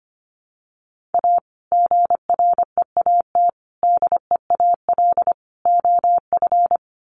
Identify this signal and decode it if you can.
Morse code, heard as a keyed tone, about 25 wpm, 700 Hz. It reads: A GREAT DEAL OF
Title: Sets of 5 Words Encore2 25wpm